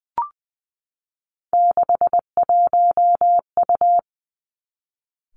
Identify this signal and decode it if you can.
Morse code, heard as a keyed tone, about 20 wpm, 700 Hz. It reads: 61U
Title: Three Challenging Letter-Number Combinations 20wpm